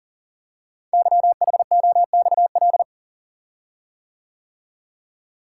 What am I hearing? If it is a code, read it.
Morse code, heard as a keyed tone, about 38 wpm, 700 Hz. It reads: YHOXL
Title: Code Groups - 5 Letters - 40wpm